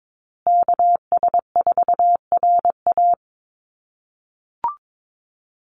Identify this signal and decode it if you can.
Morse code, heard as a keyed tone, about 22 wpm, 700 Hz. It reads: KS4RA
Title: USA Call Signs Repeated 2x 22wpm